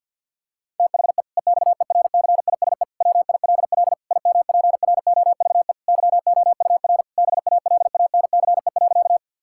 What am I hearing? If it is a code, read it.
Morse code, heard as a keyed tone, about 50 wpm, 700 Hz. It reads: THE EXERCISE WILL IMPROVE YOUR BALANCE.
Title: Sentences from Top 1000 Words - Review Part 6 - 50wpm